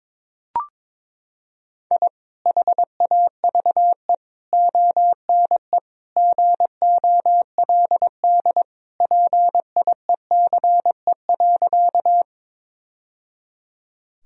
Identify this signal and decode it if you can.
Morse code, heard as a keyed tone, about 22 wpm, 700 Hz. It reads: I HAVE ONE GOLD PIECE.
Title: Sentences from Top 1000 Words - Review Part 3 - 22wpm